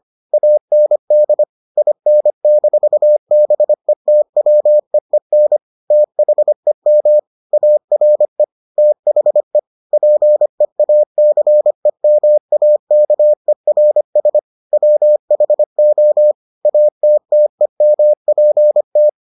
Morse code, heard as a keyed tone, about 25 wpm, 600 Hz. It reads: AND IN-BETWEEN THEM ARE THE PEACEMAKERS WHO ATTEMPT